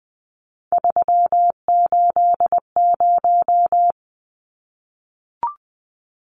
Morse code, heard as a keyed tone, about 20 wpm, 700 Hz. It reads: 380
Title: Numbers - 3-Digit 20wpm